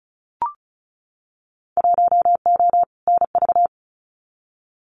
Morse code, heard as a keyed tone, about 36 wpm, 700 Hz. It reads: 1O NV
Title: Winter Field Day Exchanges Encore 35wpm